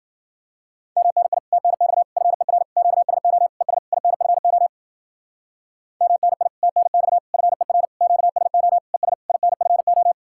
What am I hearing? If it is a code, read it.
Morse code, heard as a keyed tone, about 47 wpm, 700 Hz. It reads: MNI TNX FER QSO ES INFO MNI TNX FER QSO ES INFO
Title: Speed-Racing Common QSO Phrases Encore - 40wpm